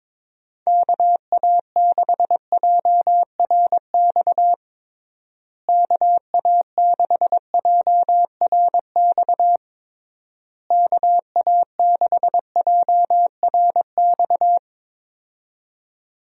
Morse code, heard as a keyed tone, about 22 wpm, 700 Hz. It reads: KA6JRX KA6JRX KA6JRX
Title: USA Call Signs - 6 Characters - Repeated 3x - 22wpm